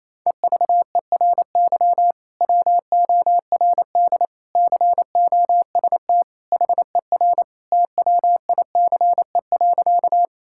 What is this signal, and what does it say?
Morse code, heard as a keyed tone, about 28 wpm, 700 Hz. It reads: EVERY WORD COST HER TWICE.
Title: Sentences from Top 1600 Words 28wpm